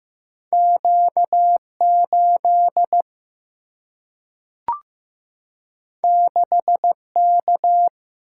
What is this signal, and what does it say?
Morse code, heard as a keyed tone, about 15 wpm, 700 Hz. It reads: Q 8 6 K
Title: Single-Letter Single-Number 15wpm